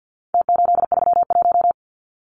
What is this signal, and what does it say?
Morse code, heard as a keyed tone, about 48 wpm, 700 Hz. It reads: T831
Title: Numbers - Contest Style - 4-Digit 50wpm